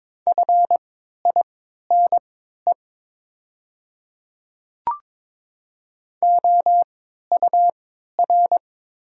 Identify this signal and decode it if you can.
Morse code, heard as a keyed tone, about 22 wpm, spaced as for 13 wpm, 700 Hz. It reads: FINE OUR
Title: 3x Character Spacing - Top 500 Words - 22wpm